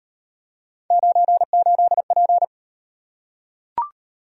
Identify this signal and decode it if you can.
Morse code, heard as a keyed tone, about 33 wpm, 700 Hz. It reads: 98P